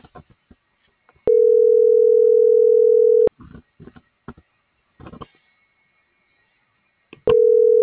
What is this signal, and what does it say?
Ringback tone (North America), 440 + 480 Hz.